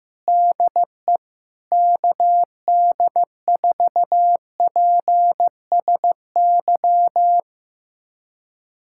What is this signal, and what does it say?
Morse code, heard as a keyed tone, about 15 wpm, 700 Hz. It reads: DE KD4PSY
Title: USA Callsign DE Callsign 15wpm